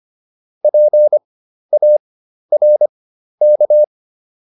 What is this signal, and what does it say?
Morse code, heard as a keyed tone, about 25 wpm, spaced as for 12 wpm, 600 Hz. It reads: PARK